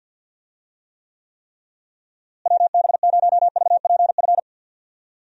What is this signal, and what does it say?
Morse code, heard as a keyed tone, about 51 wpm, 700 Hz. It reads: WB0VPF
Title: USA Call Signs - 6 Characters - Repeated 3x - 50wpm